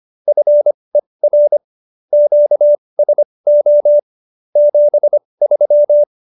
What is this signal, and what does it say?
Morse code, heard as a keyed tone, about 25 wpm, spaced as for 20 wpm, 590 Hz. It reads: FER QSO 73